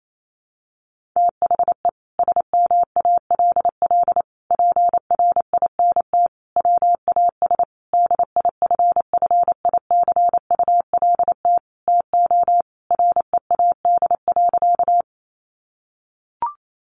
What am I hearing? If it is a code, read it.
Morse code, heard as a keyed tone, about 28 wpm, 700 Hz. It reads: THE SMALL PRINT WAS DIFFICULT TO READ.
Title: Sentences from Top 1400 Words 28wpm